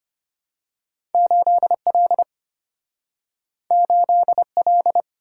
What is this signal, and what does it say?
Morse code, heard as a keyed tone, about 34 wpm, 700 Hz. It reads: 8L 8L